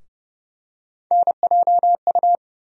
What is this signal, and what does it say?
Morse code, heard as a keyed tone, about 30 wpm, 700 Hz. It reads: NJU
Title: USA Call Sign Postfixes - 3 Letters 30wpm